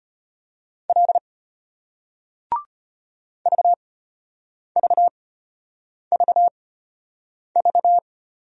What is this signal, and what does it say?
Morse code, heard as a keyed tone, about 35 wpm, spaced as for 8 wpm, 700 Hz. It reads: L VVVV